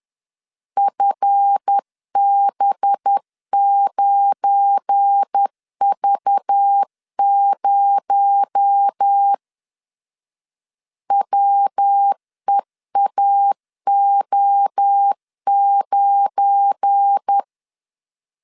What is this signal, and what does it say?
Morse code, heard as a keyed tone, about 10 wpm, 800 Hz. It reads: FB9V0 WEAO9
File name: LEZIONI / 7 GF9Z / 7 GF9 10wpm 2747